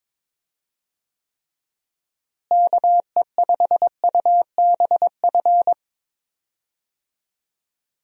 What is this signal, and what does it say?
Morse code, heard as a keyed tone, about 22 wpm, 700 Hz. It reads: KE5UBF
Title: USA Call Signs - 6 Characters - Repeated 2x - 22wpm